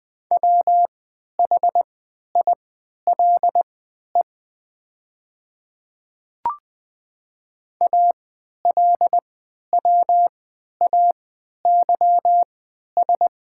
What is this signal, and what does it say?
Morse code, heard as a keyed tone, about 20 wpm, spaced as for 11 wpm, 700 Hz. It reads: WHILE ALWAYS
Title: 3x Character Spacing - Top 500 Words - 20wpm